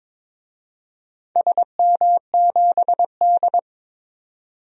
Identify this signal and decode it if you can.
Morse code, heard as a keyed tone, about 23 wpm, 700 Hz. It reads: SM7D
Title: DXCC-100 Call Signs - 4 Characters - 22wpm